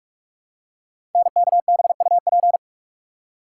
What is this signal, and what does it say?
Morse code, heard as a keyed tone, about 44 wpm, 700 Hz. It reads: NKBUP
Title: Code Groups - 5 Letters - 45wpm